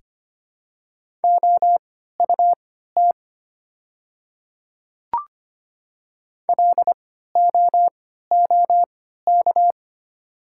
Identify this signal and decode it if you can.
Morse code, heard as a keyed tone, about 25 wpm, spaced as for 14 wpm, 700 Hz. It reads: OUT LOOK
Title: 3x Character Spacing - Top 100 Words - 25wpm